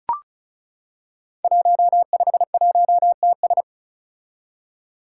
Morse code, heard as a keyed tone, about 34 wpm, 700 Hz. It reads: 151TS
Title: Five Challenging Letter-Number Combinations 35wpm